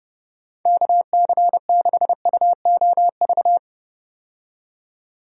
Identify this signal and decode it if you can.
Morse code, heard as a keyed tone, about 30 wpm, 700 Hz. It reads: KC6UOV
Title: USA Call Signs - 6 Characters - 30wpm